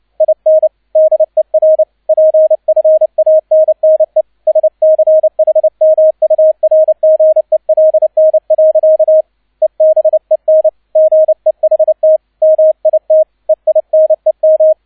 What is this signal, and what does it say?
Morse code, heard as a keyed tone, about 29 wpm, 630 Hz, with radio noise. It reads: IN DER PFANNE SCHMURGELN. EBEN GEHT MIT EINEM
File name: 002 Zweiter Streich 30wpm